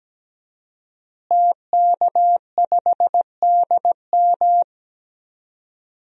Morse code, heard as a keyed tone, about 17 wpm, 700 Hz. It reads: TK5DM
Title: DXCC-100 Call Signs - 5 Characters - Repeated 2x - 17wpm